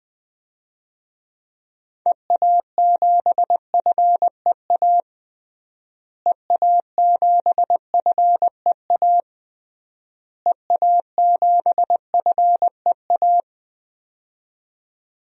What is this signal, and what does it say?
Morse code, heard as a keyed tone, about 21 wpm, 700 Hz. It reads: EA7FEA EA7FEA EA7FEA
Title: DXCC-100 Call Signs - 6 Characters - Repeated 3x - 20wpm